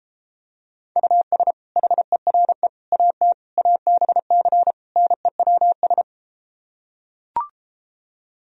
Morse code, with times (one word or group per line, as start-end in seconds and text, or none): 1.0-1.5 US
1.8-2.7 HERE
2.9-3.3 AT
3.6-4.7 ABC
5.0-6.0 NEWS